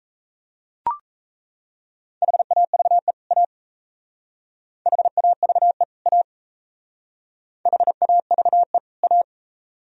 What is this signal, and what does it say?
Morse code, heard as a keyed tone, about 34 wpm, 700 Hz. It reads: HAVE A HAVE A HAVE A
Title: Speed-Racing - Sets of 2 Words 28wpm